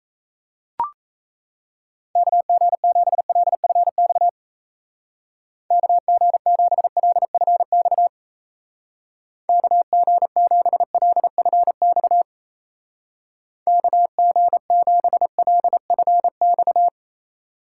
Morse code, heard as a keyed tone, about 33 wpm, 700 Hz. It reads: KG7LFX KG7LFX KG7LFX KG7LFX
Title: Speed Racing - USA 2x3 Call Signs 28wpm